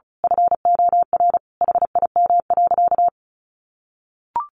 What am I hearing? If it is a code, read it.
Morse code, heard as a keyed tone, about 34 wpm, 700 Hz. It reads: FOR HIM.
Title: A Sound Of Thunder 35wpm